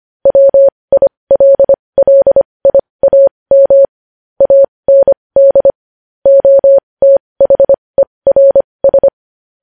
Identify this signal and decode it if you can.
Morse code, heard as a keyed tone, about 25 wpm, spaced as for 20 wpm, 550 Hz. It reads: WILLIAM AND OTHERS
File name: podcast / News headlines-1x-25x20